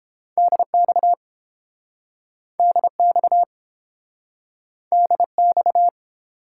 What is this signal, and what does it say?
Morse code, heard as a keyed tone, about 30 wpm, 700 Hz. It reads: DX DX DX